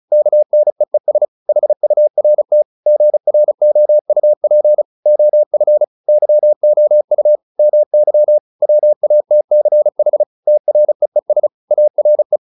Morse code, heard as a keyed tone, about 34 wpm, 600 Hz. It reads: KNEES HURT GROUP OF YOU MY WATCH TREES ARE